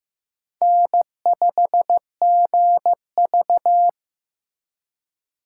Morse code, heard as a keyed tone, about 15 wpm, 700 Hz. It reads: N5GV
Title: USA Call Signs - 4 Characters - 6x Character Spacing - 15wpm